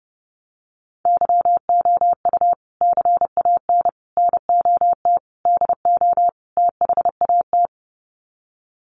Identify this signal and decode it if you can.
Morse code, heard as a keyed tone, about 30 wpm, 700 Hz. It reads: YOU CAN NOT DO THAT
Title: Sets of 5 Words 30wpm